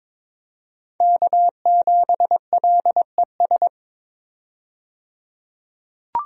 Morse code, heard as a keyed tone, about 22 wpm, 700 Hz. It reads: K7LES
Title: USA Call Signs - 5 Characters - 22wpm